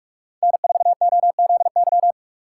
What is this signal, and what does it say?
Morse code, heard as a keyed tone, about 45 wpm, 700 Hz. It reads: N4OZY
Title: USA Call Signs - 5 Characters - 4x Character Spacing - 45wpm